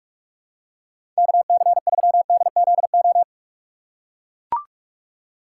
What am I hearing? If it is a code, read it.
Morse code, heard as a keyed tone, about 41 wpm, 700 Hz. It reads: KC3DZO
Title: USA Call Signs - 2x Character Spacing - 45wpm